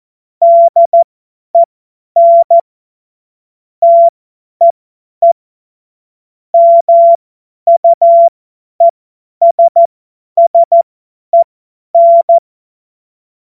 Morse code, heard as a keyed tone, about 14 wpm, spaced as for 10 wpm, 680 Hz. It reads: DEN TEE MUESSEN
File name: Morsekurs / Witze / Witz 30 Januar 2024 50 F70